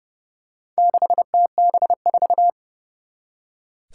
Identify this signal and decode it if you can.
Morse code, heard as a keyed tone, about 30 wpm, 700 Hz. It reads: 6TB4
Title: Four Challenging Letter-Number Combinations 30wpm